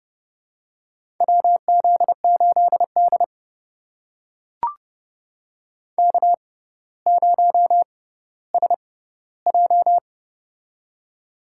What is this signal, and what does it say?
Morse code, heard as a keyed tone, about 30 wpm, spaced as for 15 wpm, 700 Hz. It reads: WZ8D K0SJ
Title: USA Call Signs - 4 Characters - 6x Character Spacing - 30wpm